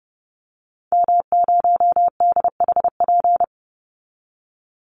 Morse code, heard as a keyed tone, about 30 wpm, 700 Hz. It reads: M0DHP
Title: Call Signs Repeated 2x - 30wpm